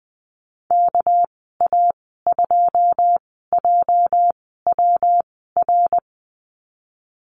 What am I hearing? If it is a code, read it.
Morse code, heard as a keyed tone, about 19 wpm, spaced as for 14 wpm, 700 Hz. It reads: KA2JWR
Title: 207 Lesson 33g - Letter j - Call Signs 20wpm